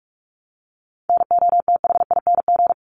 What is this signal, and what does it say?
Morse code, heard as a keyed tone, about 44 wpm, 700 Hz. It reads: NOTHING
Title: Sentences from Top 900 Words 45wpm